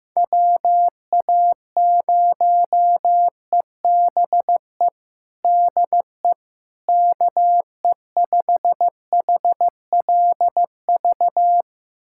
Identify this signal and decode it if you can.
Morse code, heard as a keyed tone, about 15 wpm, 700 Hz. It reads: WA0EBE DE KE5HLV
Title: USA Callsign DE Callsign - 6 Characters - 15wpm